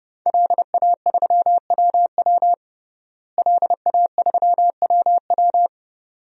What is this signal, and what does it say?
Morse code, heard as a keyed tone, about 30 wpm, 700 Hz. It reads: LA3WW LA3WW
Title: DXCC-100 Call Signs - 5 Characters - Repeated 2x - 30wpm